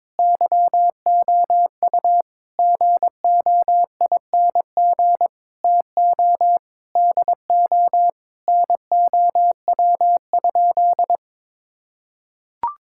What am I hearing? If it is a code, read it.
Morse code, heard as a keyed tone, about 21 wpm, 700 Hz. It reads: YOU GOING TO DO NOW?
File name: practice / 22wpm / Sentences with 3-Word Phrases - 22wpm